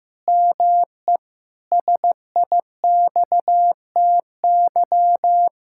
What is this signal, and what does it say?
Morse code, heard as a keyed tone, about 15 wpm, 700 Hz. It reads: ME SIXTY
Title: Pangram Challenge v2 - 15wpm